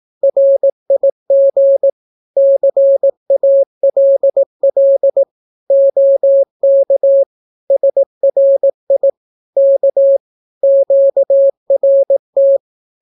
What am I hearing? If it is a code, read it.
Morse code, heard as a keyed tone, about 18 wpm, 550 Hz. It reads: RIG CALL OK SRI K QRT
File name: naukapodstaw / skroty / skroty